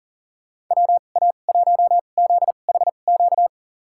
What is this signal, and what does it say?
Morse code, heard as a keyed tone, about 40 wpm, spaced as for 29 wpm, 700 Hz. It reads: WA1ZHQ
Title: USA Call Signs - 6 Characters - 2x Character Spacing - 40wpm